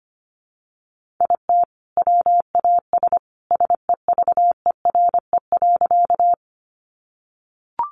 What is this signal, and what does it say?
Morse code, heard as a keyed tone, about 25 wpm, 700 Hz. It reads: IT WAS SEVERE.
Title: Sentences from Top 1400 Words 25wpm